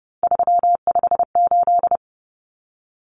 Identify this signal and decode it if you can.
Morse code, heard as a keyed tone, about 30 wpm, 700 Hz. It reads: 358
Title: Common Cut Numbers - 3-Digit 30wpm